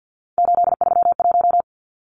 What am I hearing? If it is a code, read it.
Morse code, heard as a keyed tone, about 48 wpm, 700 Hz. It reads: 831
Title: Numbers - 3-Digit 50wpm